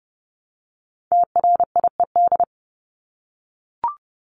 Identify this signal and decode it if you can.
Morse code, heard as a keyed tone, about 29 wpm, 700 Hz. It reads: TRIED
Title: ED Suffix - Top 100 Words with Base Words 30wpm